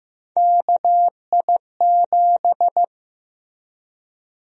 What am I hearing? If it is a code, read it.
Morse code, heard as a keyed tone, about 15 wpm, 700 Hz. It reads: KI7
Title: USA Call Sign Prefixes - 15wpm